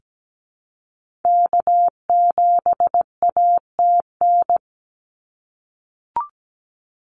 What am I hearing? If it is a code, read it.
Morse code, heard as a keyed tone, about 17 wpm, 700 Hz. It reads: K7ATN
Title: NA SOTA Activator Callsigns 17wpm